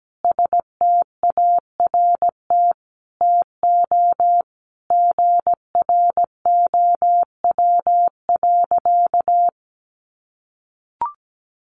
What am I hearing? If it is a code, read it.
Morse code, heard as a keyed tone, about 17 wpm, 700 Hz. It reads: START TO GROW.
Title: Easy Phrases 17wpm